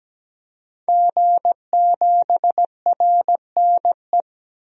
Morse code, heard as a keyed tone, about 17 wpm, 700 Hz. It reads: G7RNE
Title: DXCC-100 Call Signs - 5 Characters - 17wpm